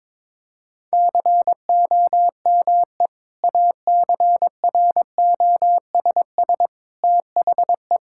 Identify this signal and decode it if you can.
Morse code, heard as a keyed tone, about 22 wpm, 700 Hz. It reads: COME ACROSS THE
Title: Sentences from Top 1000 Words - Review Part 5 - 22wpm